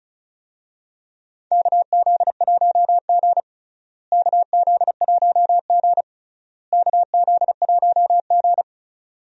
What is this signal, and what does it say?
Morse code, heard as a keyed tone, about 36 wpm, 700 Hz. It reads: KZ1G KZ1G KZ1G
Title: USA Call Signs - 4 Characters - Repeated 3x - 35wpm